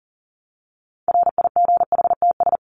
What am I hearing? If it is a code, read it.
Morse code, heard as a keyed tone, about 40 wpm, 700 Hz. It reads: RIGHTS
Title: ES:S Suffix - Top 100 Words with Base Words 40wpm